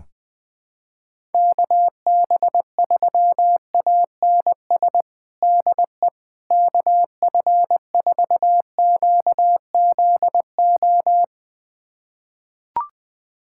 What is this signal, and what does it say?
Morse code, heard as a keyed tone, about 20 wpm, 700 Hz. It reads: KB3ANS DE KF4QZO
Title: USA Callsign DE Callsign - 6 Characters - 20wpm